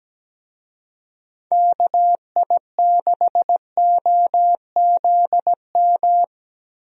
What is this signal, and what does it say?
Morse code, heard as a keyed tone, about 17 wpm, 700 Hz. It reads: KI6OZM
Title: USA Call Signs - 6 Characters - Repeated 3x - 17wpm